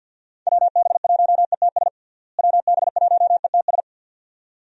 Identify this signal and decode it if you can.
Morse code, heard as a keyed tone, about 51 wpm, 700 Hz. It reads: WB1ETS WB1ETS
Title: USA Call Signs - 6 Characters - Repeated 2x - 50wpm